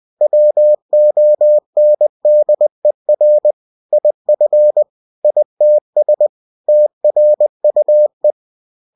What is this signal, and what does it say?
Morse code, heard as a keyed tone, about 20 wpm, 600 Hz. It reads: WONDER IF ITS TRUE
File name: cw-books / winnie-the-pooh / winnie-20wpm-page-0068